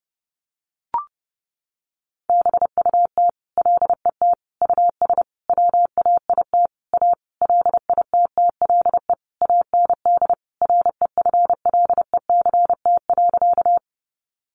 Morse code, heard as a keyed tone, about 30 wpm, 700 Hz. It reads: BUT LET US WAIT A LITTLE AND REFLECT.
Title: Sentences from Top 1500 Words 30wpm